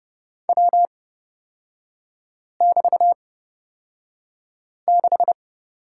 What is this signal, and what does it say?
Morse code, heard as a keyed tone, about 30 wpm, spaced as for 5 wpm, 700 Hz. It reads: W<BT>6
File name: Lesson 36e - BT - Three Characters 30wpm